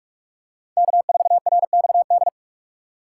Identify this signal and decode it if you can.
Morse code, heard as a keyed tone, about 46 wpm, 700 Hz. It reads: K4RXD
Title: USA Call Signs - 5 Characters - 6x Character Spacing - 45wpm